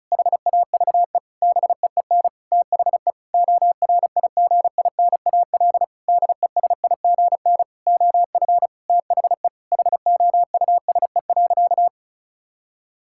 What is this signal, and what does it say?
Morse code, heard as a keyed tone, about 36 wpm, 700 Hz. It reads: HAVE BEEN THE ORIGINAL DESIGN OF THE HOUSE.
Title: Sentences from Top 1000 Words - Review Part 6 - 35wpm